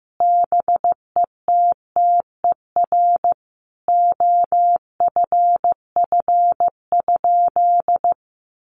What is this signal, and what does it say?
Morse code, heard as a keyed tone, about 14 wpm, 700 Hz. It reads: BETTER OFF?
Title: Sentences from Top 500 Words 15wpm